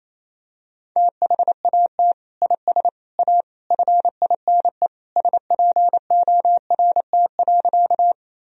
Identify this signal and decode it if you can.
Morse code, heard as a keyed tone, about 28 wpm, 700 Hz. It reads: THAT IS A FINE SPORT.
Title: Sentences from Top 1000 Words - Review Part 4 - 28wpm